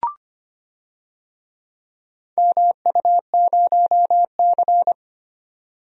Morse code, heard as a keyed tone, about 25 wpm, 700 Hz. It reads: MU0C
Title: DXCC-100 Call Signs - 4 Characters - 25wpm